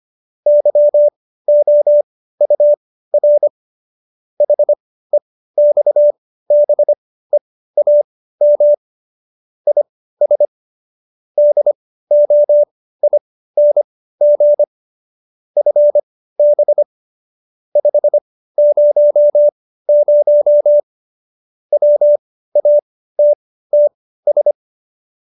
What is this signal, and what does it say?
Morse code, heard as a keyed tone, about 25 wpm, spaced as for 15 wpm, 590 Hz. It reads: YOUR HEXBEAM IS DOING FB 500 WATTS
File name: qso222 15